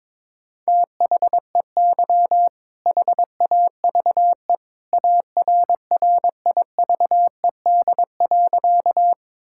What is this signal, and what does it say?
Morse code, heard as a keyed tone, about 22 wpm, 700 Hz. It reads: THEY HAVE ARRIVED.
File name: practice / 22wpm / Sentences with 2-Word Phrases - 22wpm